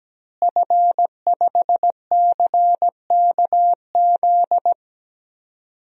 Morse code, heard as a keyed tone, about 17 wpm, 700 Hz. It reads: F5CKZ
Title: DXCC-100 Call Signs - 5 Characters - Repeated 2x - 17wpm